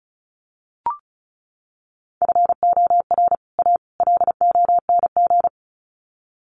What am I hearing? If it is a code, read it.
Morse code, heard as a keyed tone, about 34 wpm, 700 Hz. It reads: FOR A LONG
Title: Sets of 3 Words 35wpm